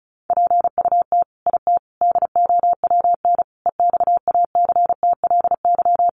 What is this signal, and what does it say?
Morse code, heard as a keyed tone, about 34 wpm, 700 Hz. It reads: PUT IT DOWN EXACTLY
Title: Sentences from Top 900 Words 35wpm